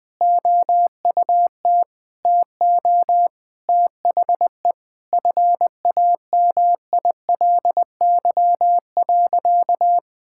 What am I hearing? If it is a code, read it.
Morse code, heard as a keyed tone, about 20 wpm, 700 Hz. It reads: OUT TO THE FAMILY.
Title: Sentences with 8-Word Phrases - 20wpm